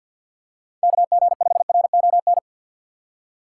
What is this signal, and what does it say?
Morse code, heard as a keyed tone, about 49 wpm, 700 Hz. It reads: KG5RON
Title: USA Call Signs - 6 Characters - 6x Character Spacing - 50wpm